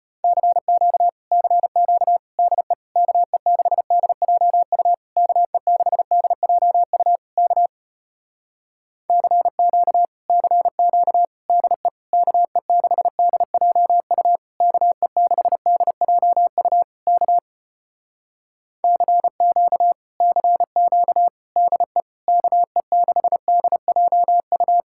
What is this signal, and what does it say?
Morse code, heard as a keyed tone, about 33 wpm, 700 Hz. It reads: CQ CQ DE KE6DJU KE6DJU K CQ CQ DE KE6DJU KE6DJU K CQ CQ DE KE6DJU